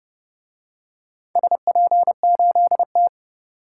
Morse code, heard as a keyed tone, about 29 wpm, 700 Hz. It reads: SP8T